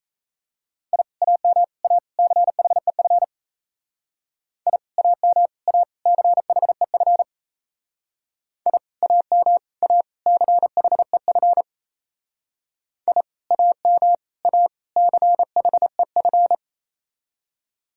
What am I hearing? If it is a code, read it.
Morse code, heard as a keyed tone, about 38 wpm, 700 Hz. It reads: I AM A CHEF I AM A CHEF I AM A CHEF I AM A CHEF